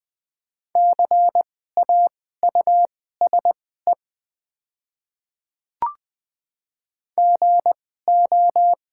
Morse code, heard as a keyed tone, about 20 wpm, spaced as for 14 wpm, 700 Hz. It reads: CAUSE GO
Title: 2x Character Spacing - Top 200 Words - 20wpm